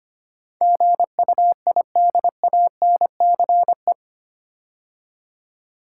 Morse code, heard as a keyed tone, about 25 wpm, 700 Hz. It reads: GUIDANCE
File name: 8 Letter Words - First 3 Letters Given - 25wpm